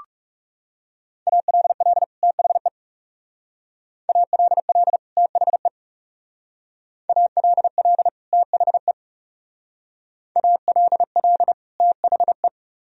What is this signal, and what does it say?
Morse code, heard as a keyed tone, about 40 wpm, 700 Hz. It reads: ALL THE ALL THE ALL THE ALL THE